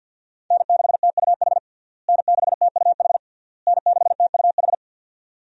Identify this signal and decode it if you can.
Morse code, heard as a keyed tone, about 48 wpm, 700 Hz. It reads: N6TUH N6TUH N6TUH
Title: USA Call Signs - 5 Characters - Repeated 3x - 50wpm